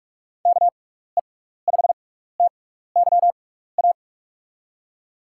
Morse code, heard as a keyed tone, about 46 wpm, spaced as for 16 wpm, 700 Hz. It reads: KE5TYA
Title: USA Call Signs - 6 Characters - 6x Character Spacing - 45wpm